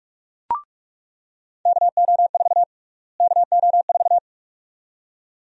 Morse code, heard as a keyed tone, about 42 wpm, 700 Hz. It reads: KO4 KO4
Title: USA Call Sign Prefixes - Repeated 2x - 45wpm